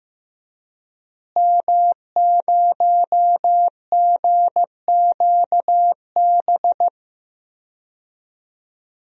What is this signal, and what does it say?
Morse code, heard as a keyed tone, about 15 wpm, 700 Hz. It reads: M0GQB
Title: DXCC-100 Call Signs - 5 Characters - Repeated 3x - 15wpm